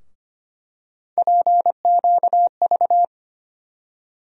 Morse code, heard as a keyed tone, about 25 wpm, 700 Hz. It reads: PQV
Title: USA Call Sign Postfixes - 3 Letters 25wpm